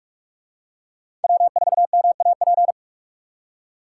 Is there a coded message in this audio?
Yes: Morse code from 1.2 to 2.7 s, heard as a keyed tone, about 42 wpm, 700 Hz. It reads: W4MAP